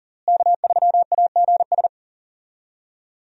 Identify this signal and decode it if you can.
Morse code, heard as a keyed tone, about 39 wpm, 700 Hz. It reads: K3AGS
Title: USA Call Signs - 5 Characters - Repeated 3x - 40wpm